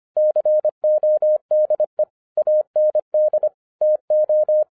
Morse code, heard as a keyed tone, about 25 wpm, 610 Hz. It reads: CODE AND TO